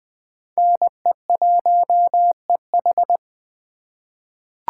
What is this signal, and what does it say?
Morse code, heard as a keyed tone, about 20 wpm, 700 Hz. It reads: NE1EH
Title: USA Call Signs - 5 Characters - 2x Character Spacing - 20wpm